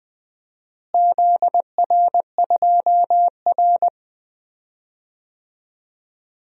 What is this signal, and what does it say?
Morse code, heard as a keyed tone, about 20 wpm, 700 Hz. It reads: ZR2R
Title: DXCC-100 Call Signs - 4 Characters - Repeated 3x - 20wpm